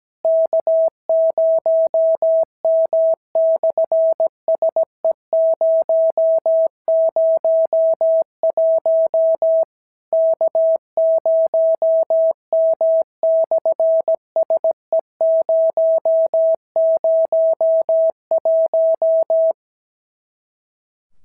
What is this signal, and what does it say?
Morse code, heard as a keyed tone, about 17 wpm, 650 Hz. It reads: K0M/SE001 K0M/SE001
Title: SOTA QSO Practice 17wpm